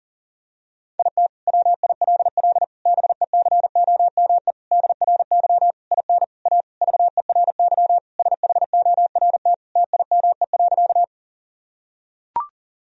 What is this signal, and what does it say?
Morse code, heard as a keyed tone, about 37 wpm, 700 Hz. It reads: IT WILL BECOME DRY IN A VERY SHORT TIME.
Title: Sentences from Top 1000 Words - Review Part 3 - 40wpm